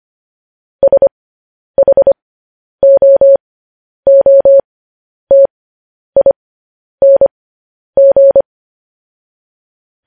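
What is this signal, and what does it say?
Morse code, heard as a keyed tone, about 25 wpm, spaced as for 10 wpm, 550 Hz. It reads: SHOOTING
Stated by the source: Morse code practice sending news headlines at 10 wpm.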